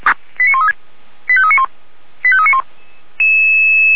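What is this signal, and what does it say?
Five-tone selective call (ZVEI), heard as quick runs of five tones, tones of about 70 ms. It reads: 89136 85391 85391
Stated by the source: ZVEI five-tone selective call received by SDR.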